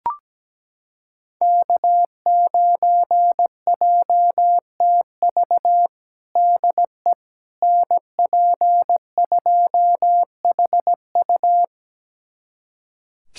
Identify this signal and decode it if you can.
Morse code, heard as a keyed tone, about 17 wpm, 700 Hz. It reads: K9JTV DE NP2HU
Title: USA Callsign DE Callsign - 5 Characters - 17wpm